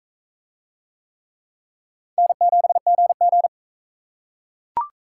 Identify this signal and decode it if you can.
Morse code, heard as a keyed tone, about 37 wpm, 700 Hz. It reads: N7GG